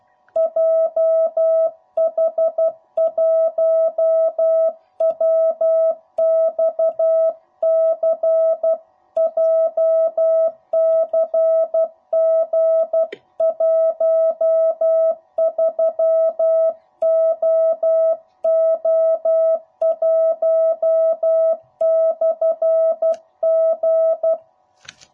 Morse code, heard as a keyed tone, about 12 wpm, 660 Hz. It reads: JH1WXC JCG13OO1/G
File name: jh8cbh / 2021cw / cw05b